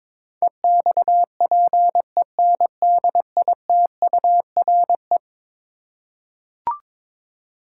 Morse code, heard as a keyed tone, about 21 wpm, 700 Hz. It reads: EXPENDITURE
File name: practice / 22wpm / Eleven Letter Words - 22wpm